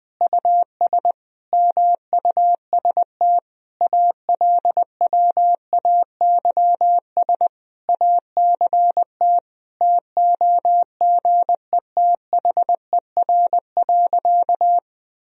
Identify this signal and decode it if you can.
Morse code, heard as a keyed tone, about 20 wpm, 700 Hz. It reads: US MUST ALWAYS ACT TOGETHER.
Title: Sentences from Top 1000 Words - Review Part 1 - 20wpm